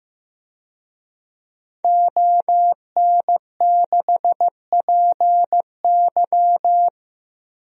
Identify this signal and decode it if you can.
Morse code, heard as a keyed tone, about 15 wpm, 700 Hz. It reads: ON6PY
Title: DXCC-100 Call Signs - 5 Characters - Repeated 2x - 15wpm